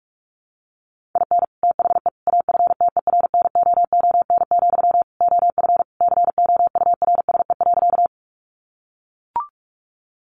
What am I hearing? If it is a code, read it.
Morse code, heard as a keyed tone, about 44 wpm, 700 Hz. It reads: IN THE AFTERNOON, OF COURSE.
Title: Sentences from Top 1500 Words 45wpm